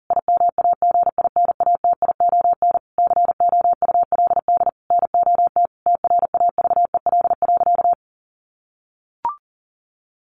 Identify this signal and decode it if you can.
Morse code, heard as a keyed tone, about 40 wpm, 700 Hz. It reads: IMAGINATION COULD NOT TRAVEL.
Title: Sentences from Top 1300 Words 40wpm